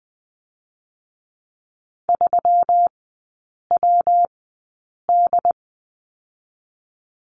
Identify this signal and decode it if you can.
Morse code, heard as a keyed tone, about 20 wpm, 700 Hz. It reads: 3 W D
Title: 121 Lesson 20e - Number 3 - Three Characters 20wpm